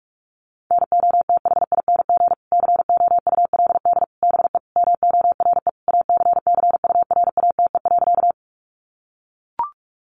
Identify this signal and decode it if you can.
Morse code, heard as a keyed tone, about 44 wpm, 700 Hz. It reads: NOTHING COULD BE MORE ACCURATE.
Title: Sentences from Top 1200 Words 45wpm